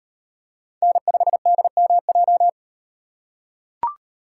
Morse code, heard as a keyed tone, about 38 wpm, 700 Hz. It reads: N5DMJ